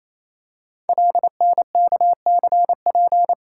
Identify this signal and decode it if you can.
Morse code, heard as a keyed tone, about 27 wpm, 700 Hz. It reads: LNKCP
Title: Code Groups - 5 Letter:Numbers - 28wpm